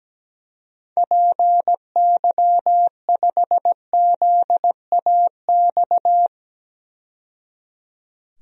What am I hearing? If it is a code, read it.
Morse code, heard as a keyed tone, about 17 wpm, 700 Hz. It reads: PY5ZAX
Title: DXCC-100 Call Signs - 6 Characters - 17wpm